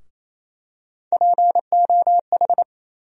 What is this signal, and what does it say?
Morse code, heard as a keyed tone, about 28 wpm, 700 Hz. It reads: POH
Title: USA Call Sign Postfixes - 3 Letters 28wpm